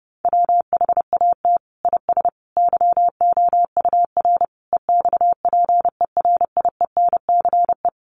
Morse code, heard as a keyed tone, about 30 wpm, 700 Hz. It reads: WHAT IS YOUR EXPERIENCE
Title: Sentences from Top 600 Words 30wpm